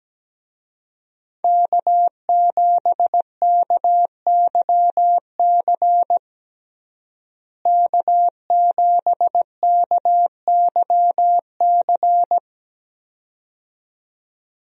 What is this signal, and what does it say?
Morse code, heard as a keyed tone, about 17 wpm, 700 Hz. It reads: K7KYC K7KYC
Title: USA Call Signs - 5 Characters - Repeated 2x - 17wpm